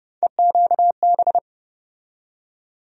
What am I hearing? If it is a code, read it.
Morse code, heard as a keyed tone, about 30 wpm, 700 Hz. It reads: EQB
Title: USA Call Sign Postfixes - 3 Letters 30wpm